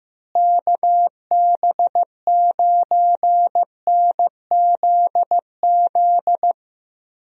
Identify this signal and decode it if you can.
Morse code, heard as a keyed tone, about 16 wpm, 700 Hz. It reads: KB9NZZ
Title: USA Call Signs - 6 Characters - Repeated 2x - 15wpm